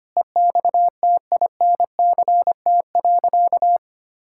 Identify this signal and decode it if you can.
Morse code, heard as a keyed tone, about 25 wpm, 700 Hz. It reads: EXTINCT.
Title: Pangram Challenge v2 - 25wpm